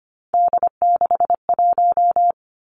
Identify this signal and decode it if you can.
Morse code, heard as a keyed tone, about 25 wpm, 700 Hz. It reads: D61
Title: Three Challenging Letter-Number Combinations 25wpm